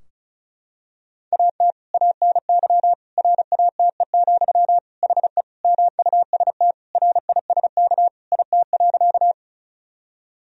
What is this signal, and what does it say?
Morse code, heard as a keyed tone, about 34 wpm, 700 Hz. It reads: AT ANY RATE, HE MUST RISK IT.
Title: Sentences from Top 1000 Words - Review Part 4 - 35wpm